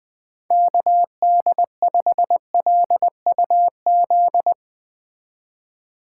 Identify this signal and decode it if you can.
Morse code, heard as a keyed tone, about 20 wpm, 700 Hz. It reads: KD5LUZ